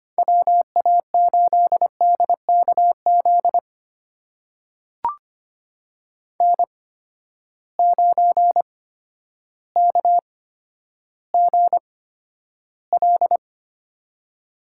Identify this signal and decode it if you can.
Morse code, heard as a keyed tone, about 25 wpm, spaced as for 11 wpm, 700 Hz. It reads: WA8DKZ N9KGL
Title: USA Call Signs - 8x Character Spacing - 25wpm